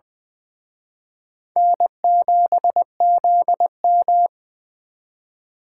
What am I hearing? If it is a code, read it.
Morse code, heard as a keyed tone, about 23 wpm, 700 Hz. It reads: N7ZM
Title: Speed Racing - USA 1x2 Call Signs 20wpm